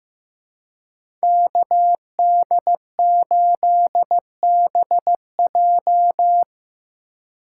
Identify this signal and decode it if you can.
Morse code, heard as a keyed tone, about 17 wpm, 700 Hz. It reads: KD8BJ
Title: Speed Racing - USA 2x2 Call Signs 15wpm